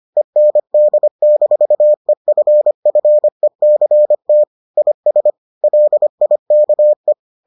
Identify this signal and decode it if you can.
Morse code, heard as a keyed tone, about 25 wpm, 600 Hz. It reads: END-EFFECT IS LIKE